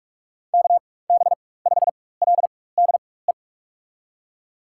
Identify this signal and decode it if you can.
Morse code, heard as a keyed tone, about 46 wpm, spaced as for 21 wpm, 700 Hz. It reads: KB5LDE
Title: USA Call Signs - 6 Characters - 4x Character Spacing - 45wpm